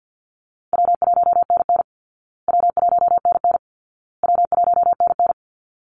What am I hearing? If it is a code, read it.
Morse code, heard as a keyed tone, about 49 wpm, 700 Hz. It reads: W1NN W1NN W1NN
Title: Easy Call Signs Repeated 3x - 50wpm